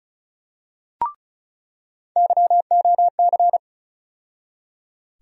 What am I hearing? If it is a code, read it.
Morse code, heard as a keyed tone, about 36 wpm, 700 Hz. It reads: YOC
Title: USA Call Sign Postfixes - 3 Letters 35wpm